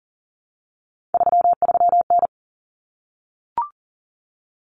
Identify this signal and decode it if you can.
Morse code, heard as a keyed tone, about 40 wpm, 700 Hz. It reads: 33N
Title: Common Cut Numbers - 3-Digit 40wpm